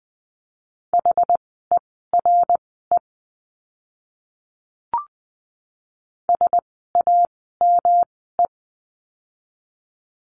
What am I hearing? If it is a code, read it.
Morse code, heard as a keyed tone, about 20 wpm, spaced as for 14 wpm, 700 Hz. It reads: HERE SAME
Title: 206 Lesson 33f - Letter j - Words 20wpm